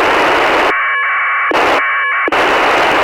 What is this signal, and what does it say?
Packet radio (AX.25, as in APRS).
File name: ISS-01